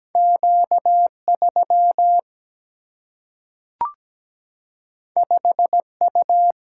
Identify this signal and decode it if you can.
Morse code, heard as a keyed tone, about 17 wpm, 700 Hz. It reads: Q 3 5 U
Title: Single-Letter Single-Number 17wpm